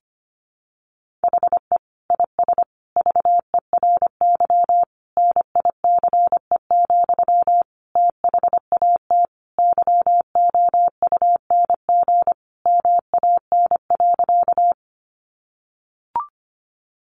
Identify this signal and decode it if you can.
Morse code, heard as a keyed tone, about 25 wpm, 700 Hz. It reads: HE IS VERY NICE, THAT YOUNG MAN.
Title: Sentences from Top 1100 Words 25wpm